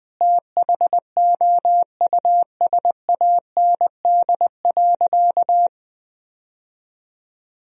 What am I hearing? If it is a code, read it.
Morse code, heard as a keyed tone, about 21 wpm, 700 Hz. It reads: THOUSAND.
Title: Sentences from Top 1000 Words - Review Part 3 - 20wpm